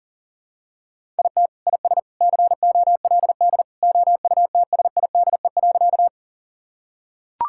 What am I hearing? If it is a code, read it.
Morse code, heard as a keyed tone, about 40 wpm, 700 Hz. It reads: IT IS COLD OUTSIDE.
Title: Sentences from Top 1000 Words - Review Part 6 - 40wpm